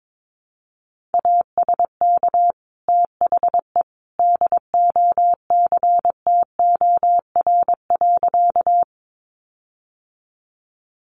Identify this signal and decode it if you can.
Morse code, heard as a keyed tone, about 22 wpm, 700 Hz. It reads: ASK THE DOCTOR.
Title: Sentences from Top 1500 Words 22wpm